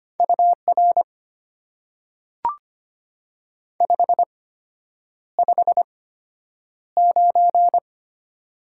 Morse code, heard as a keyed tone, about 25 wpm, spaced as for 9 wpm, 690 Hz. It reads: UR 559
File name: Top 90 QSO Elements - 8x Character Spacing - 25wpm